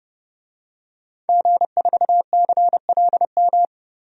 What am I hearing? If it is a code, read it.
Morse code, heard as a keyed tone, about 31 wpm, 700 Hz. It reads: G4CLM
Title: DXCC-100 Call Signs - 5 Characters - 30wpm